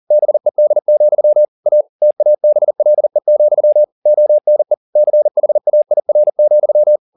Morse code, heard as a keyed tone, about 40 wpm, 600 Hz. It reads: BED, A TABLE, ONE CHAIR,